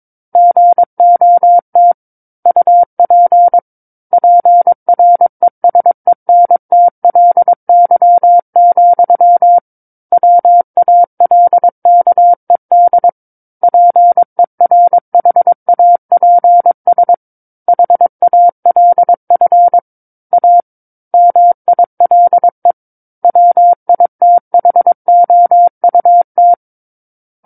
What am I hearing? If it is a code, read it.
Morse code, heard as a keyed tone, about 22 wpm, 700 Hz. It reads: GOT UP PRESENTLY, WALKED PERHAPS HALF A MILE WITHOUT
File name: Book One / 13-Chapter 13 - 22 WPM